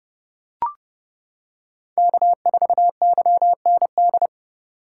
Morse code, heard as a keyed tone, about 29 wpm, 700 Hz. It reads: K4YND
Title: K1USN SST QSO Practice - 30wpm